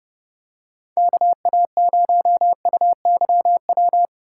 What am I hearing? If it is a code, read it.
Morse code, heard as a keyed tone, about 29 wpm, 700 Hz. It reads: KA0UYW